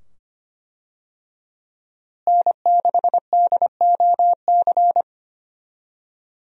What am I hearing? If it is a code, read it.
Morse code, heard as a keyed tone, about 25 wpm, 700 Hz. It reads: N6DOC
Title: USA Call Signs - 5 Characters - Repeated 2x - 25wpm